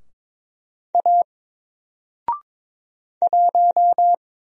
Morse code, heard as a keyed tone, about 22 wpm, 700 Hz. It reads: A 1
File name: Single-Letter-Number-22wpm